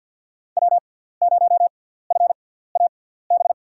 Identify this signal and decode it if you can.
Morse code, heard as a keyed tone, about 50 wpm, spaced as for 17 wpm, 700 Hz. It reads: W0FAB
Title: USA Call Signs - 5 Characters - 6x Character Spacing - 50wpm